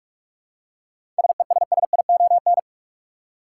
Morse code, heard as a keyed tone, about 47 wpm, 700 Hz. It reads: SESSION
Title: Seven Letter Words - 45wpm